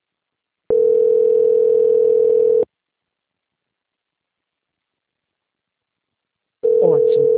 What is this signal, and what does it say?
Ringback tone (North America), 440 + 480 Hz.
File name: out- -2007- - - .3588